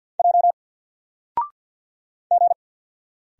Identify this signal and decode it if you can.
Morse code, heard as a keyed tone, about 49 wpm, 700 Hz. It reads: J G
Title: Single Character (Letter:Number:Punctuation:Prosign) Alt 50wpm